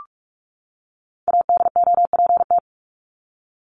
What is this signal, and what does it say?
Morse code, heard as a keyed tone, about 44 wpm, 700 Hz. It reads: ADOPT
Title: ED Suffix - Top 100 Words with Base Words 45wpm